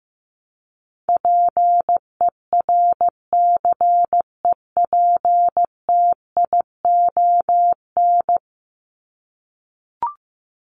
Morse code, heard as a keyed tone, about 15 wpm, 700 Hz. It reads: PERCEPTION
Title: Top 1751-2000 Words 15wpm